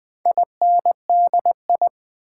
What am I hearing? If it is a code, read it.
Morse code, heard as a keyed tone, about 20 wpm, 700 Hz. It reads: INDI
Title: Build A Word - 4 Letters At A Time - 10-Letter Words - 20wpm